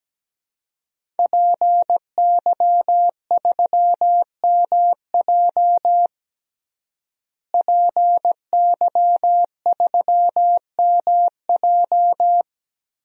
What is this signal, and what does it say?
Morse code, heard as a keyed tone, about 17 wpm, 700 Hz. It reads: PY3MJ PY3MJ
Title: DXCC-100 Call Signs - 5 Characters - Repeated 2x - 17wpm